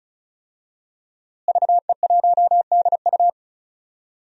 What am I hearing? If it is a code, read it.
Morse code, heard as a keyed tone, about 34 wpm, 700 Hz. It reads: VE1DU
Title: DXCC-100 Call Signs - 5 Characters - Repeated 2x - 35wpm